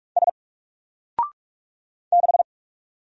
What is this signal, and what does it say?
Morse code, heard as a keyed tone, about 46 wpm, 700 Hz. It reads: S 6
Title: Single Letter-Number 45wpm